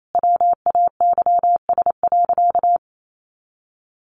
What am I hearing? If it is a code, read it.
Morse code, heard as a keyed tone, about 28 wpm, 700 Hz. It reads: WAYS.
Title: A Sound Of Thunder 28wpm